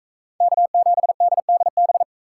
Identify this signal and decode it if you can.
Morse code, heard as a keyed tone, about 41 wpm, 700 Hz. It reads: K7DDB